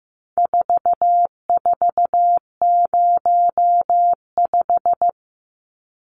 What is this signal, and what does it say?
Morse code, heard as a keyed tone, about 15 wpm, 700 Hz. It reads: 4405
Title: Numbers - 4-Digit 15wpm